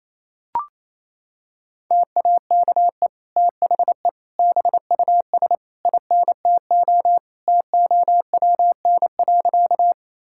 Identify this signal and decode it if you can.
Morse code, heard as a keyed tone, about 28 wpm, 700 Hz. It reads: TAKE THE BUS INTO TOWN.
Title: Sentences from Top 1000 Words - Review Part 6 - 28wpm